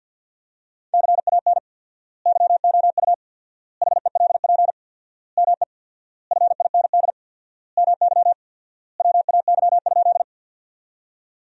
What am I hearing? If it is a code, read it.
Morse code, heard as a keyed tone, about 50 wpm, 700 Hz. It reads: CAN YOU HELP ME FIND MY WAY?
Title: 4x Word-Spacing - Sentences from Top 200 Words - 50wpm